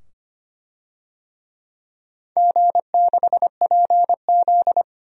Morse code, heard as a keyed tone, about 25 wpm, 700 Hz. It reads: G6PZ
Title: DXCC-100 Call Signs - 4 Characters - 25wpm